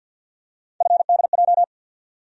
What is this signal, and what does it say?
Morse code, heard as a keyed tone, about 47 wpm, 700 Hz. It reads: FDJ
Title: USA Call Sign Postfixes - 3 Letters 50wpm